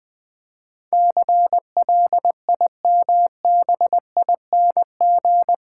Morse code, heard as a keyed tone, about 20 wpm, 700 Hz. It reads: CLIMBING
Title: Difficult Words with Hints - 20wpm